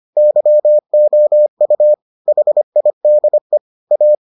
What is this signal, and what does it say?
Morse code, heard as a keyed tone, about 25 wpm, 600 Hz. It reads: YOU HIDE A